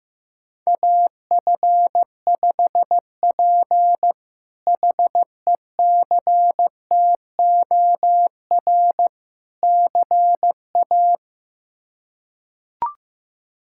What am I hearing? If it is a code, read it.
Morse code, heard as a keyed tone, about 15 wpm, 700 Hz. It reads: AF5P HECTOR CA
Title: K1USN SST QSO Practice - 15wpm